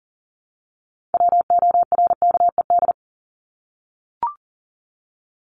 Morse code, heard as a keyed tone, about 40 wpm, 700 Hz. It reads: WORKED
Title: ED Suffix - Top 100 Words with Base Words 40wpm